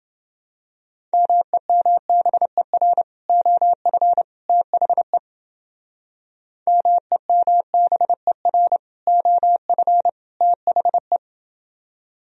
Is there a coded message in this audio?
Yes: Morse code from 1.1 to 11.2 s, heard as a keyed tone, about 30 wpm, 700 Hz. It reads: MEMBER OF THE MEMBER OF THE